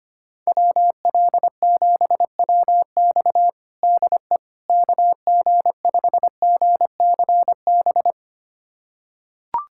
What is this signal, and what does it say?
Morse code, heard as a keyed tone, about 25 wpm, 700 Hz. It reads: WL7WX DE KG5GCB
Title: USA Callsign DE Callsign 25wpm